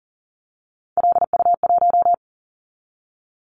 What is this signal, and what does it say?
Morse code, heard as a keyed tone, about 43 wpm, 700 Hz. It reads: LU1
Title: Callsign Prefixes for Top 50 DXCC - 40wpm